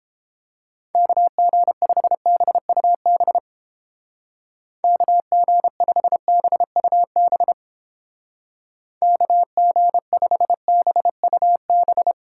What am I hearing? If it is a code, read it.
Morse code, heard as a keyed tone, about 29 wpm, 700 Hz. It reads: KG5BUB KG5BUB KG5BUB